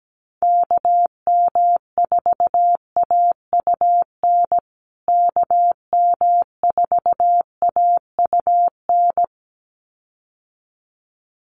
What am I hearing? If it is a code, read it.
Morse code, heard as a keyed tone, about 17 wpm, 700 Hz. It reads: KM4AUN KM4AUN
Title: USA Call Signs Repeated 2x 17wpm